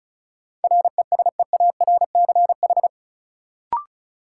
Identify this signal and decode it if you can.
Morse code, heard as a keyed tone, about 36 wpm, 700 Hz. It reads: RESEARCH
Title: Sentences from Top 1400 Words - New Word Drill Alt - 35wpm